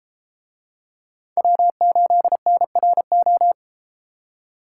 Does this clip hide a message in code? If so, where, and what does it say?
Yes: Morse code from 1.4 to 3.5 s, heard as a keyed tone, about 33 wpm, 700 Hz. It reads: W8NRO